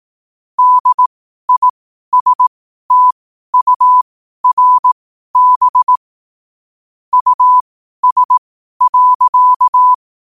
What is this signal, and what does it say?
Morse code, heard as a keyed tone, about 18 wpm, spaced as for 12 wpm, 1000 Hz. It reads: DISTURB US.
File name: qotdfiles / 12wpm